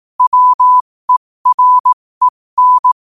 Morse code, heard as a keyed tone, about 18 wpm, spaced as for 15 wpm, 1000 Hz. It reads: WEREN
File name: qotdfiles / 15wpm